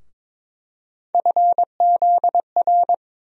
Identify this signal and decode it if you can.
Morse code, heard as a keyed tone, about 22 wpm, 700 Hz. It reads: FZR
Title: USA Call Sign Postfixes - 3 Letters 22wpm